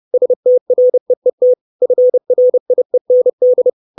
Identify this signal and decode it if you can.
Morse code, heard as a keyed tone, about 30 wpm, 500 Hz. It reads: STREET FRIEND
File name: 632 Words 30WPM